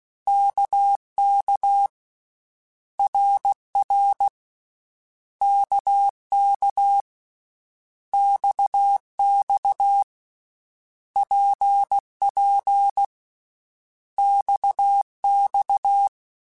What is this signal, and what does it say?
Morse code, heard as a keyed tone, about 16 wpm, 800 Hz. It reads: KK RR KK XX PP XX
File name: Basic Letter Groups By7Groups / RKPX / RKPX@13